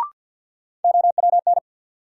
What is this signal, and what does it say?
Morse code, heard as a keyed tone, about 52 wpm, 700 Hz. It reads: OWN
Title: Sentences from Top 300 Words - New Word Drill - 50wpm